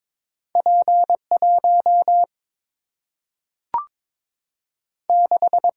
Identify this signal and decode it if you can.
Morse code, heard as a keyed tone, about 22 wpm, 700 Hz. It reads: P 1 6
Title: Single-Letter Single-Number 22wpm